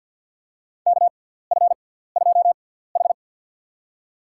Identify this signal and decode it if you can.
Morse code, heard as a keyed tone, about 52 wpm, spaced as for 17 wpm, 700 Hz. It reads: KF2H
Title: USA Call Signs - 4 Characters - 6x Character Spacing - 50wpm